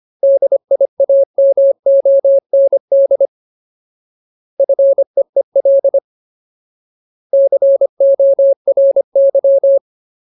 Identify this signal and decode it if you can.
Morse code, heard as a keyed tone, about 25 wpm, 550 Hz. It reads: DIAMOND FEEL CORY
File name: Ultimate Word List / 25WPM Ultimate